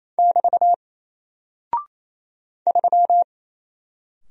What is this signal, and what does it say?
Morse code, heard as a keyed tone, about 27 wpm, 700 Hz. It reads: <BT> 3
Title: Single Character (Letter:Number:Punctuation:Prosign) 28wpm